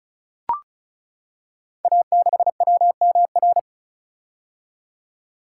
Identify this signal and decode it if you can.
Morse code, heard as a keyed tone, about 36 wpm, 700 Hz. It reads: A6WMR
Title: Code Groups - 5 Letter:Numbers - 35wpm